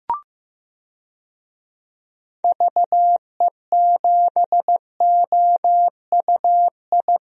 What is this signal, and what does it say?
Morse code, heard as a keyed tone, about 15 wpm, 700 Hz. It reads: VE7OUI
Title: DXCC-100 Call Signs - 6 Characters - Repeated 3x - 15wpm